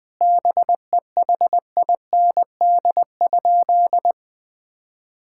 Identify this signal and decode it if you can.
Morse code, heard as a keyed tone, about 20 wpm, 700 Hz. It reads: BEHIND?
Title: Sentences from Top 1000 Words - Review Part 5 - 20wpm